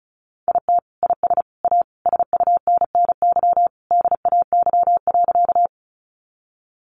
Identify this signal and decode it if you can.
Morse code, heard as a keyed tone, about 34 wpm, 700 Hz. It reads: IT IS A SUNNY DAY.
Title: Easy Phrases 35wpm